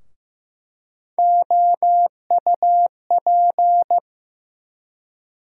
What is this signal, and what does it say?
Morse code, heard as a keyed tone, about 15 wpm, 700 Hz. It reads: OUP
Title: USA Call Sign Postfixes - 3 Letters 15wpm